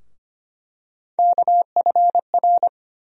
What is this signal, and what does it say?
Morse code, heard as a keyed tone, about 25 wpm, 700 Hz. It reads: KFR
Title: USA Call Sign Postfixes - 3 Letters 25wpm